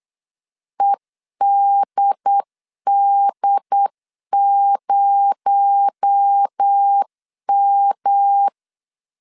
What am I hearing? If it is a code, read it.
Morse code, heard as a keyed tone, about 8 wpm, 800 Hz. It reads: EDD0M
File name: CW / DET / 1 6 1500C S C 8wpm 5219